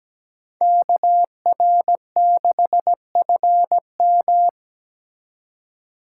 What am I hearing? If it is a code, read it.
Morse code, heard as a keyed tone, about 17 wpm, 700 Hz. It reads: KR6FM
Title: USA Call Signs - 5 Characters - 2x Character Spacing - 17wpm